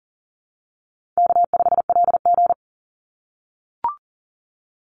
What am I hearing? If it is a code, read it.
Morse code, heard as a keyed tone, about 40 wpm, 700 Hz. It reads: K5LG
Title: Call Signs - 40wpm